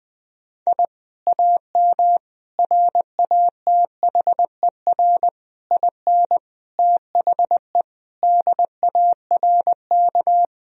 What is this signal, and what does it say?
Morse code, heard as a keyed tone, about 19 wpm, 700 Hz. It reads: I AM RATHER IN THE DARK
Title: Sentences from Top 1000 Words - Review Part 4 - 20wpm